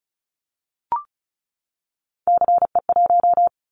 Morse code, heard as a keyed tone, about 34 wpm, 700 Hz. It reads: CE1
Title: Callsign Prefixes for Top 100 DXCC - 35wpm